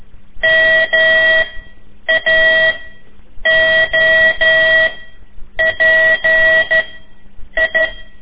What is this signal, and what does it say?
Morse code, heard as a keyed tone, about 12 wpm, 1950 Hz, with noise. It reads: M A O P I
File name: ik1weg2 / CORSI CW / lez4-3